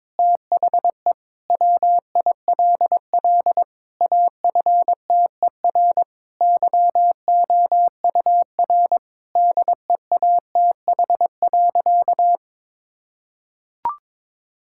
Morse code, heard as a keyed tone, about 22 wpm, 700 Hz. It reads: THE WILL AFTER YOUR DEATH.
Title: Sentences from Top 1600 Words 22wpm